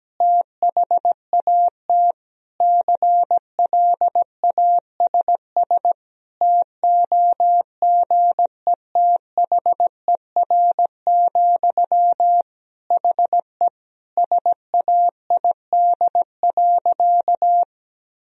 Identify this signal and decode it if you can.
Morse code, heard as a keyed tone, about 17 wpm, 700 Hz. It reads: THAT CLASS TOGETHER, HE SAID.
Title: Sentences from Top 1000 Words - Review Part 2 - 17wpm